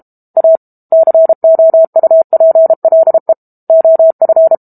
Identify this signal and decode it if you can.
Morse code, heard as a keyed tone, about 32 wpm, 650 Hz. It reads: A COUPLE OF
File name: Book Two / 06-Chapter 6 - 32 WPM